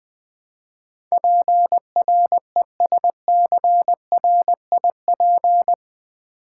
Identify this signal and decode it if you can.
Morse code, heard as a keyed tone, about 21 wpm, 700 Hz. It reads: PRESCRIP
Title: Build A Word - 4 Letters At A Time - 12-Letter Words - 20wpm